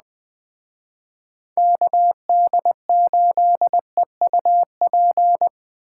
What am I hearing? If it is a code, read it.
Morse code, heard as a keyed tone, about 20 wpm, 700 Hz. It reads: KD8EUP